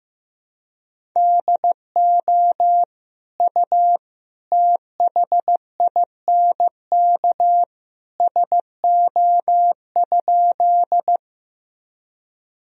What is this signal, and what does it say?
Morse code, heard as a keyed tone, about 15 wpm, 700 Hz. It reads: DO U THINK SO?
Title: Sentences from Top 200 Words and Abbreviations 15wpm